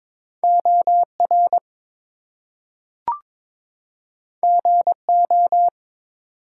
Morse code, heard as a keyed tone, about 22 wpm, 700 Hz. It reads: OR GO
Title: Two Letter Words - 22wpm